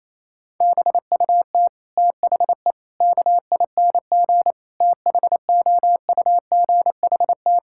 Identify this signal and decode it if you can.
Morse code, heard as a keyed tone, about 28 wpm, 700 Hz. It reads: BUT THE KING THOUGHT
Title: Sentences from Top 1600 Words 28wpm